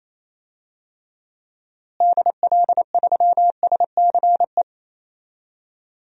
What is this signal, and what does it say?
Morse code, heard as a keyed tone, about 28 wpm, 700 Hz. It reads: DL3SCE
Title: DXCC-100 Call Signs - 6 Characters - Repeated 3x - 28wpm